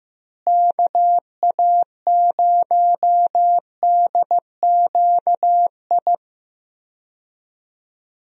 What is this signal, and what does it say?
Morse code, heard as a keyed tone, about 16 wpm, 700 Hz. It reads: KA0DQI
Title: USA Call Signs - 6 Characters - Repeated 2x - 15wpm